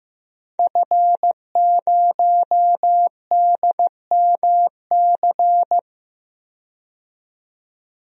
Morse code, heard as a keyed tone, about 15 wpm, 700 Hz. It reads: F0DMC
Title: DXCC-100 Call Signs - 5 Characters - 15wpm